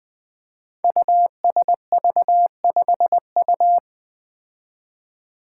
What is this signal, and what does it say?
Morse code, heard as a keyed tone, about 20 wpm, 700 Hz. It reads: USV5U
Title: Five Challenging Letter-Number Combinations 20wpm